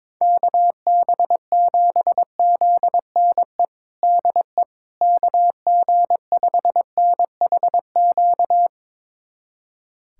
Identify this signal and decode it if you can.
Morse code, heard as a keyed tone, about 22 wpm, 700 Hz. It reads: KB7ZNE DE KG5NHQ
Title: USA Callsign DE Callsign 22wpm